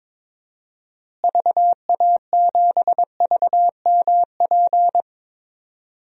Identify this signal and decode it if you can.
Morse code, heard as a keyed tone, about 22 wpm, 700 Hz. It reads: VA7VMP
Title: DXCC-100 Call Signs - 6 Characters - Repeated 2x - 22wpm